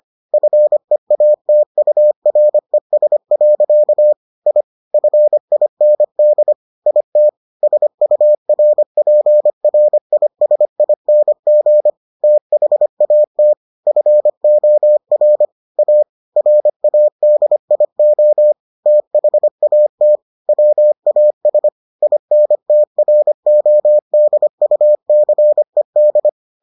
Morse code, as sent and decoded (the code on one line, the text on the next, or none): ..-. .- - ..- .-. ... .-.-.- / .. / ..-. .. -. -.. / .. - / ... ..- .-. .--. .-. .. ... .. -. --. / - .... .- - / ..-. --- .-. / .- / .-. .- -.. .. --- / - .... .- - / .-- .- ... / .. -. - .-. --- -.. ..- -.-. -..
FEATURES. I FIND IT SURPRISING THAT FOR A RADIO THAT WAS INTRODUCED